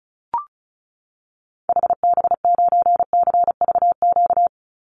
Morse code, heard as a keyed tone, about 34 wpm, 700 Hz. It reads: HB9CVQ
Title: Call Signs Repeated 2x - 35wpm